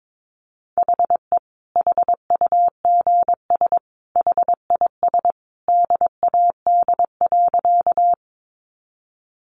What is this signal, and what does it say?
Morse code, heard as a keyed tone, about 22 wpm, 700 Hz. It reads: HE HUGS HIS DAD.
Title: Easy Phrases 22wpm